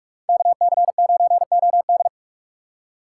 Morse code, heard as a keyed tone, about 41 wpm, 700 Hz. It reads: KC9OD
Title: USA Call Signs - 5 Characters - 2x Character Spacing - 45wpm